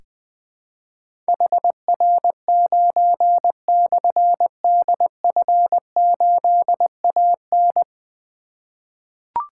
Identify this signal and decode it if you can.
Morse code, heard as a keyed tone, about 20 wpm, 700 Hz. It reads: HR9/DF8AN
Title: Difficult Call Signs 20wpm